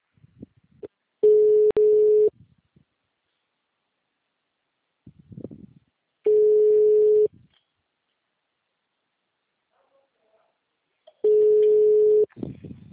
Ringback tone (Europe, 425 Hz).